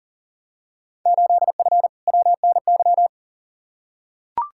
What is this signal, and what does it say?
Morse code, heard as a keyed tone, about 40 wpm, 700 Hz. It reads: 8F WNY
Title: Field Day Exchanges Encore - 40wpm